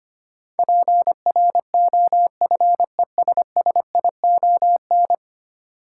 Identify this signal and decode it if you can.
Morse code, heard as a keyed tone, about 25 wpm, 700 Hz. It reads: PROFESSION
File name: Ten Letter Words - 25wpm